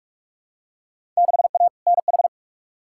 Morse code, heard as a keyed tone, about 42 wpm, 700 Hz. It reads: 6A NH
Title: Field Day Exchanges - 45wpm